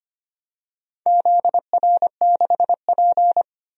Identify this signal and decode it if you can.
Morse code, heard as a keyed tone, about 25 wpm, 700 Hz. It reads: ZR6P
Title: DXCC-100 Call Signs - 4 Characters - Repeated 3x - 25wpm